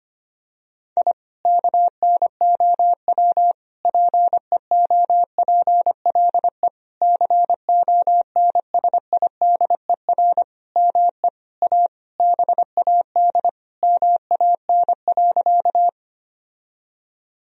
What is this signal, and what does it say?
Morse code, heard as a keyed tone, about 25 wpm, 700 Hz. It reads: I KNOW PEOPLE CONSIDER ME A BAD MAN.
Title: Sentences from Top 1000 Words - Review Part 4 - 25wpm